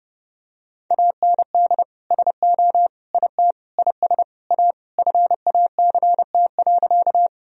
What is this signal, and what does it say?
Morse code, heard as a keyed tone, about 30 wpm, 700 Hz. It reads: AND SO IT IS A FACT.
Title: Sentences from Top 1000 Words - Review Part 3 - 30wpm